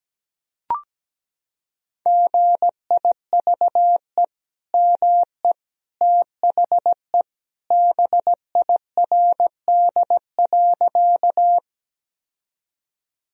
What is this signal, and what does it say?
Morse code, heard as a keyed tone, about 17 wpm, 700 Hz. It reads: GIVE ME THE BIRD.
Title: Sentences from Top 1000 Words - Review Part 2 - 17wpm